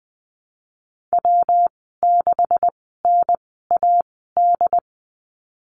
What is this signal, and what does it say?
Morse code, heard as a keyed tone, about 20 wpm, spaced as for 14 wpm, 700 Hz. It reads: W6NAD
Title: 128 Lesson 21g - Number 6 - Call Signs 20wpm